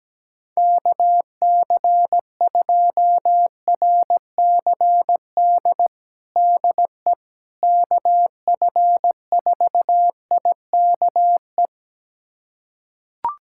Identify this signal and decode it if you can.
Morse code, heard as a keyed tone, about 17 wpm, 700 Hz. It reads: KC2RCD DE KF4IKE
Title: USA Callsign DE Callsign 17wpm